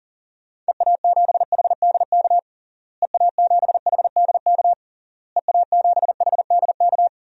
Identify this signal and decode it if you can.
Morse code, heard as a keyed tone, about 37 wpm, 700 Hz. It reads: EA7HDK EA7HDK EA7HDK
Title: DXCC-100 Call Signs - 6 Characters - Repeated 3x - 40wpm